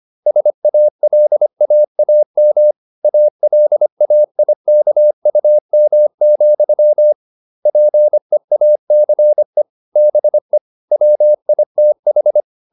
Morse code, heard as a keyed tone, about 25 wpm, 600 Hz. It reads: SALAAM ALAIKUM, PEACE BE WITH